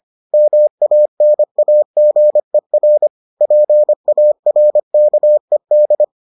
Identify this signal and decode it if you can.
Morse code, heard as a keyed tone, about 25 wpm, 600 Hz. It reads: MANAGER PARKED